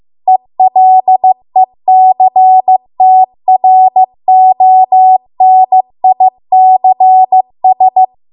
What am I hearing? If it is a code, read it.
Morse code, heard as a keyed tone, about 15 wpm, 750 Hz. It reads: ELECTRONICS
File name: Archive / 15 WPM / 15WPM